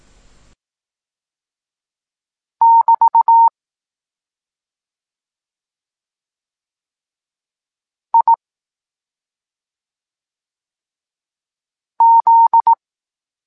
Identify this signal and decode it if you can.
Morse code, heard as a keyed tone, about 18 wpm, 920 Hz. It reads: <BT> I Z
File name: Tekst-26 1